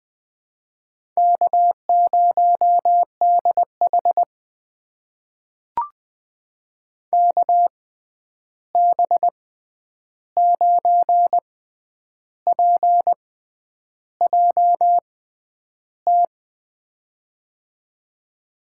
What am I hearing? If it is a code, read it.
Morse code, heard as a keyed tone, about 20 wpm, spaced as for 9 wpm, 700 Hz. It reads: K0DH KB9PJT
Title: USA Call Signs - 6x Character Spacing - 20wpm